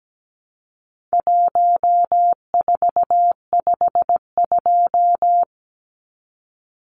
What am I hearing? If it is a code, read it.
Morse code, heard as a keyed tone, about 17 wpm, 700 Hz. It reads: 1452
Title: Numbers - Contest Style - 4-Digit 17wpm